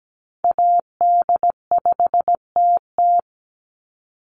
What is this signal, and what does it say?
Morse code, heard as a keyed tone, about 17 wpm, 700 Hz. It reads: AD5TT
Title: Call Signs - 17wpm